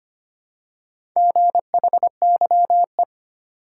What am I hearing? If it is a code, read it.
Morse code, heard as a keyed tone, about 25 wpm, 700 Hz. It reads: GHYE
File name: practice / 25wpm / Four Letter-Number Combinations - 25wpm